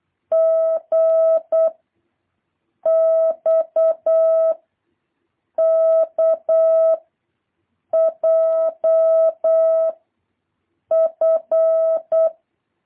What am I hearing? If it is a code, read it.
Morse code, heard as a keyed tone, about 8 wpm, 650 Hz. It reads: G X K J F
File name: jh8cbh / 2023cw